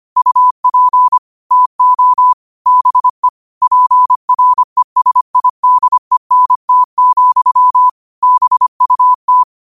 Morse code, heard as a keyed tone, about 25 wpm, 1000 Hz. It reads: UP TO BE PRESIDENT, BUT
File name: qotdfiles / 25wpm